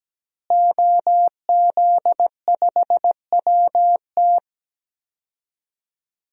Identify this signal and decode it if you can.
Morse code, heard as a keyed tone, about 17 wpm, 700 Hz. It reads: OZ5WT
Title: DXCC-100 Call Signs - 5 Characters - Repeated 3x - 17wpm